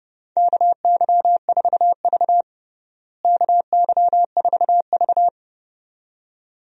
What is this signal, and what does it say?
Morse code, heard as a keyed tone, about 32 wpm, 700 Hz. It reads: KY4V KY4V
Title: USA Call Signs - 4 Characters - Repeated 2x - 30wpm